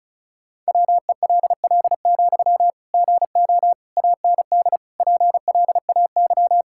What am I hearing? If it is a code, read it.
Morse code, heard as a keyed tone, about 35 wpm, 700 Hz. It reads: WELL, GO AND PLAY
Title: Sentences from Top 1600 Words 35wpm